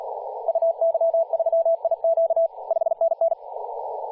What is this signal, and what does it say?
Morse code, heard as a keyed tone, about 38 wpm, 660 Hz, with radio noise. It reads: UY3IQ 5NN
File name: posts / 2016-july-20-0522z-t2cow-18-071mhz-cw